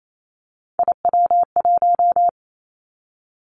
Morse code, heard as a keyed tone, about 28 wpm, 700 Hz. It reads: IW1
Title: Callsign Prefixes for Top 50 DXCC - 28wpm